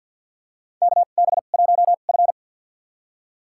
Morse code, heard as a keyed tone, about 48 wpm, spaced as for 35 wpm, 700 Hz. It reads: KB1F
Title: USA Call Signs - 4 Characters - 2x Character Spacing - 50wpm